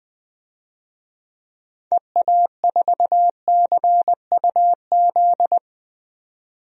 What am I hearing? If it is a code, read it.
Morse code, heard as a keyed tone, about 20 wpm, 700 Hz. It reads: EA4CUZ
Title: DXCC-100 Call Signs - 6 Characters - 20wpm